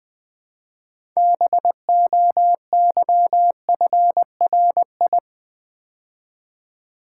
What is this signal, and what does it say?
Morse code, heard as a keyed tone, about 20 wpm, 700 Hz. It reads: BOYFRI
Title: Build A Word - 3 Letters At A Time - 9-Letter Words - 20wpm